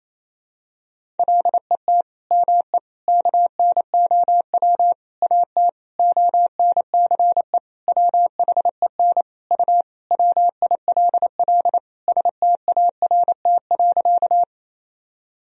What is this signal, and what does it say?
Morse code, heard as a keyed tone, about 28 wpm, 700 Hz. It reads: LET ME KNOW AT ONCE WHEN U WILL START.
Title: Sentences from Top 300 Words and Abbreviations 28wpm